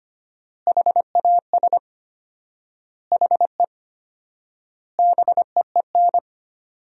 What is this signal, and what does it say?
Morse code, heard as a keyed tone, about 25 wpm, 700 Hz. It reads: HAS HE BEEN
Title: 4x Word-Spacing - Sentences from Top 100 Words - 25wpm